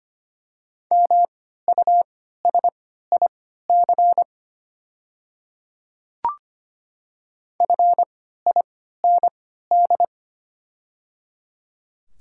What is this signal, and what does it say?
Morse code, heard as a keyed tone, about 25 wpm, spaced as for 14 wpm, 700 Hz. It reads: MUSIC FIND
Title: 3x Character Spacing - Top 300 Words - 25wpm